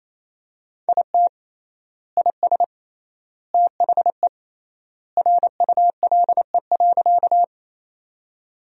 Morse code, heard as a keyed tone, about 28 wpm, 700 Hz. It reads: IT IS THE RULE.
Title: 3x Word-Spacing - Sentences from Top 400 Words - 28wpm